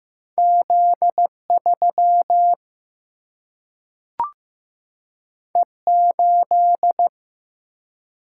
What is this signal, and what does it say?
Morse code, heard as a keyed tone, about 15 wpm, 700 Hz. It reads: Z 3 E 8
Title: Single-Letter Single-Number 15wpm